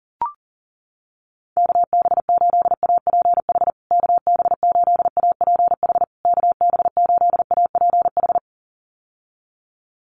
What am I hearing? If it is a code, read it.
Morse code, heard as a keyed tone, about 40 wpm, 700 Hz. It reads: KB8APH KB8APH KB8APH
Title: USA Call Signs Repeated 3x 40wpm